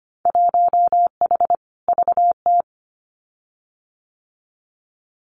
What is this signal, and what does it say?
Morse code, heard as a keyed tone, about 25 wpm, 700 Hz. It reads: 1H VT
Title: Winter Field Day Exchanges 25wpm